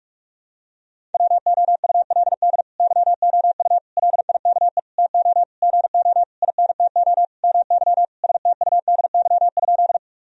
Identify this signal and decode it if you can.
Morse code, heard as a keyed tone, about 41 wpm, 700 Hz. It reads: WOULD YOU LIKE TO GO INTO MY STUDY?
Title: Sentences from Top 1000 Words - Review Part 1 - 45wpm